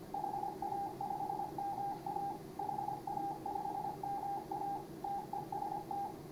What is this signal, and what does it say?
Morse code, heard as a keyed tone, about 28 wpm, 800 Hz, with radio noise. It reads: HA5MA HA5MA TEST
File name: CW / HA5MA RHOB HANG CW 2017